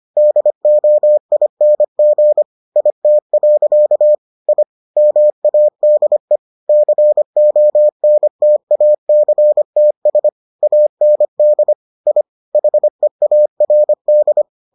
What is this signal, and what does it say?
Morse code, heard as a keyed tone, about 25 wpm, 600 Hz. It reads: DOING IT. I MADE CONTACTS AND I HEARD